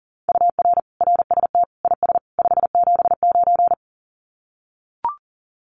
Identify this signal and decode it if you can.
Morse code, heard as a keyed tone, about 40 wpm, 700 Hz. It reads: UR RST IS 579
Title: Common QSO Phrases 40wpm